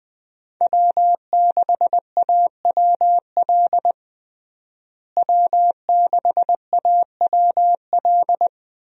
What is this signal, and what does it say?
Morse code, heard as a keyed tone, about 20 wpm, 700 Hz. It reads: W6AWL W6AWL
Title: USA Call Signs - 5 Characters - Repeated 2x - 20wpm